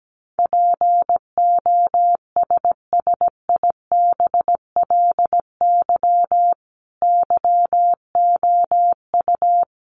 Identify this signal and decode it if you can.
Morse code, heard as a keyed tone, about 17 wpm, 700 Hz. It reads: POSSIBLY YOU
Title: Sentences from Top 1300 Words 17wpm